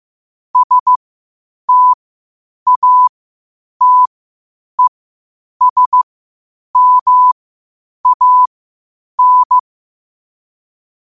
Morse code, heard as a keyed tone, about 15 wpm, spaced as for 8 wpm, 1000 Hz. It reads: STATESMAN
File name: qotdfiles / 8wpm